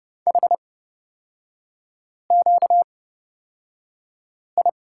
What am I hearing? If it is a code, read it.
Morse code, heard as a keyed tone, about 30 wpm, spaced as for 5 wpm, 700 Hz. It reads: HQI
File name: Lesson 38e - Letter q - Three Characters 30wpm